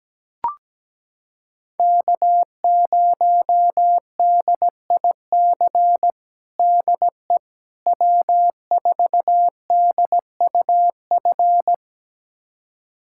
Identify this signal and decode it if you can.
Morse code, heard as a keyed tone, about 17 wpm, 700 Hz. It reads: K0DIC DE W4DUF
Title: USA Callsign DE Callsign - 5 Characters - 17wpm